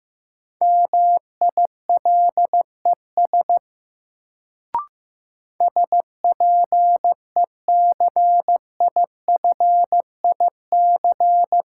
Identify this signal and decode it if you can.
Morse code, heard as a keyed tone, about 15 wpm, 700 Hz. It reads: MILES SPECIFIC
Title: Sentences from Top 1000 Words - Review Part 3 - New Word Drill - 15wpm